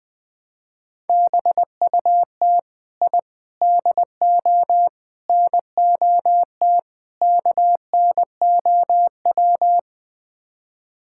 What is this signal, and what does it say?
Morse code, heard as a keyed tone, about 20 wpm, 700 Hz. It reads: BUT I DO NOT KNOW
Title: Build A Phrase - 5-Word Phrases - 20wpm